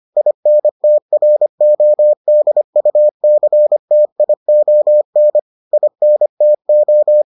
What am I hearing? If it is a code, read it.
Morse code, heard as a keyed tone, about 25 wpm, 600 Hz. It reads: INTRODUCTION INTO